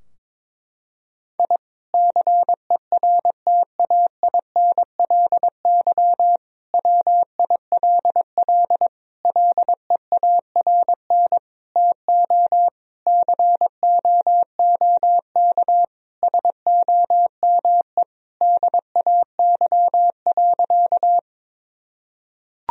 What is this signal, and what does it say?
Morse code, heard as a keyed tone, about 22 wpm, 700 Hz. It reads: I CERTAINLY WILL LEARN TO COOK SOME DAY.
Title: Sentences from Top 1000 Words - Review Part 4 - 22wpm